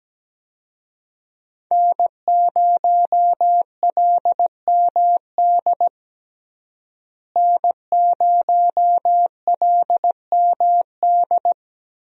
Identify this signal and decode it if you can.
Morse code, heard as a keyed tone, about 17 wpm, 700 Hz. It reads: N0LMD N0LMD
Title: USA Call Signs - 5 Characters - Repeated 2x - 17wpm